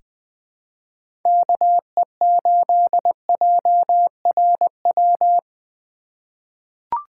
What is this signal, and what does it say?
Morse code, heard as a keyed tone, about 20 wpm, 700 Hz. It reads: KE8JRW
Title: USA Call Signs - 2x Character Spacing - 20wpm